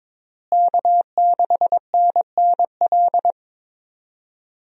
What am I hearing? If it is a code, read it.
Morse code, heard as a keyed tone, about 22 wpm, 700 Hz. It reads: K6NNL
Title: USA Call Signs - 5 Characters - Repeated 3x - 22wpm